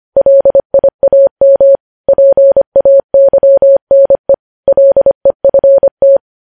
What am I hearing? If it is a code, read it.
Morse code, heard as a keyed tone, about 25 wpm, 550 Hz. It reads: LIAM PAYNE LEFT
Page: news headlines in morse code at 25 wpm